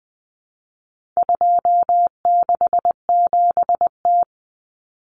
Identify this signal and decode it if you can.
Morse code, heard as a keyed tone, about 20 wpm, 700 Hz. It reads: 267T
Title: Common Cut Numbers - 4-Digit 20wpm